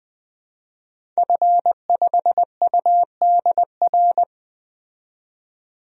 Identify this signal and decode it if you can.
Morse code, heard as a keyed tone, about 20 wpm, 700 Hz. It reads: F5UDR
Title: DXCC-100 Call Signs - 5 Characters - 20wpm